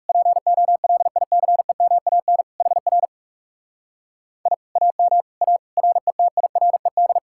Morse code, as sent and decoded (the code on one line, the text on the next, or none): .--. --- .-.. .. -.-. -- .- -. / .... .-. / .. / .- -- / .- / .-. - .. .-. -..
POLICEMAN HR I AM A RETIRED